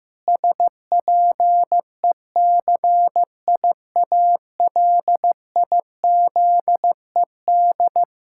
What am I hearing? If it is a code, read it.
Morse code, heard as a keyed tone, about 14 wpm, 700 Hz. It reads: SPECIALIZED
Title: Pangram Challenge v2 - 15wpm